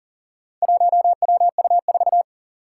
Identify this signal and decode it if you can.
Morse code, heard as a keyed tone, about 40 wpm, 700 Hz. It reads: 1WU4
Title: Four Challenging Letter-Number Combinations 40wpm